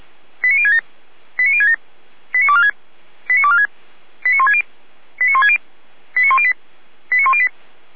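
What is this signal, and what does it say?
Five-tone selective call (ZVEI), heard as quick runs of five tones, tones of about 70 ms. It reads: 89067 89067 89246 89246 89160 89160 89108 89108
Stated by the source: ZVEI five-tone selective call 89072 received on an SDR.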